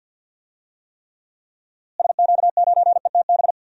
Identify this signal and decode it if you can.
Morse code, heard as a keyed tone, about 52 wpm, 700 Hz. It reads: SQ9ETB
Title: DXCC-100 Call Signs - 6 Characters - Repeated 2x - 50wpm